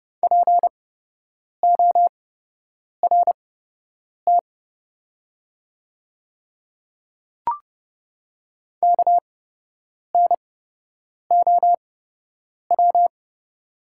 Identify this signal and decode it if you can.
Morse code, heard as a keyed tone, about 30 wpm, spaced as for 8 wpm, 700 Hz. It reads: PORT KNOW
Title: 8x Character Spacing - Top 300 Words - 30wpm